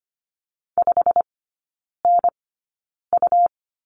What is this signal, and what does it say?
Morse code, heard as a keyed tone, about 25 wpm, spaced as for 9 wpm, 700 Hz. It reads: 5NU
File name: Lesson 17e - Letter u - Three Characters 25wpm